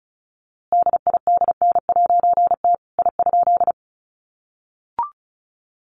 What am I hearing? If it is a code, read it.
Morse code, heard as a keyed tone, about 34 wpm, 700 Hz. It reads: DIDN'T I?
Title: A Sound Of Thunder 35wpm